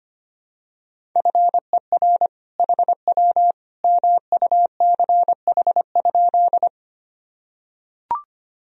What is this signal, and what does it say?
Morse code, heard as a keyed tone, about 25 wpm, 700 Hz. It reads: FER HW MUCH?
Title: Sentences from Top 200 Words and Abbreviations 25wpm